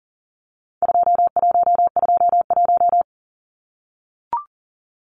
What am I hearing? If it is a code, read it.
Morse code, heard as a keyed tone, about 40 wpm, 700 Hz. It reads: 2121
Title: Numbers - 4-Digit 40wpm